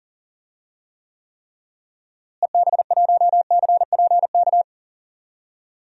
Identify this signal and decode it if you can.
Morse code, heard as a keyed tone, about 40 wpm, 700 Hz. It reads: EB1CPK
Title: DXCC-100 Call Signs - 6 Characters - Repeated 3x - 40wpm